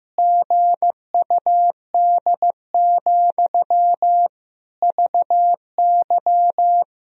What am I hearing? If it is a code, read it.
Morse code, heard as a keyed tone, about 14 wpm, 700 Hz. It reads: GUD, VY
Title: Sentences from Top 200 Words and Abbreviations 15wpm